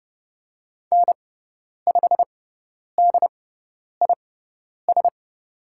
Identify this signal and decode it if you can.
Morse code, heard as a keyed tone, about 30 wpm, spaced as for 10 wpm, 700 Hz. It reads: N5DIS
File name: Lesson 18g - Letter m - Call Signs 30wpm